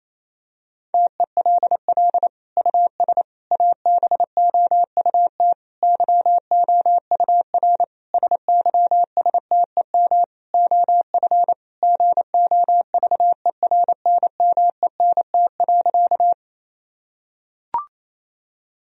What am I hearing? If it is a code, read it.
Morse code, heard as a keyed tone, about 28 wpm, 700 Hz. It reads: TELL US ABOUT YOUR SYSTEM OF GOVERNMENT.
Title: Sentences from Top 1000 Words - Review Part 3 - 28wpm